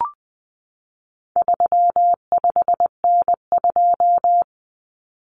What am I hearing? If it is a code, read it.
Morse code, heard as a keyed tone, about 20 wpm, 700 Hz. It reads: 35N2
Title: Common Cut Numbers - 4-Digit 20wpm